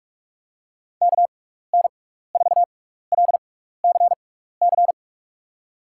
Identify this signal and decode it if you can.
Morse code, heard as a keyed tone, about 46 wpm, spaced as for 16 wpm, 700 Hz. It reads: KN4LCC
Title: USA Call Signs - 6 Characters - 6x Character Spacing - 45wpm